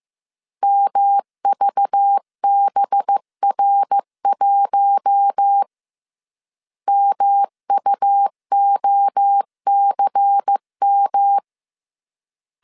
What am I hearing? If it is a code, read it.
Morse code, heard as a keyed tone, about 15 wpm, 800 Hz. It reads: MVBR1 MUOCM